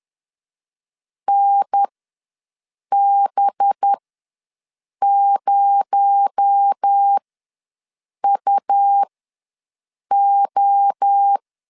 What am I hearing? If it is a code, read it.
Morse code, heard as a keyed tone, about 10 wpm, spaced as for 6 wpm, 800 Hz. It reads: NB0UO
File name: LEZIONI / 7 GF9Z / 7 GF ws 10wpm 3918